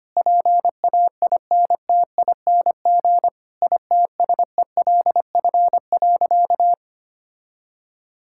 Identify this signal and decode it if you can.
Morse code, heard as a keyed tone, about 25 wpm, 700 Hz. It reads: PAINTING ITSELF.
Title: Sentences from Top 1000 Words - Review Part 6 - 25wpm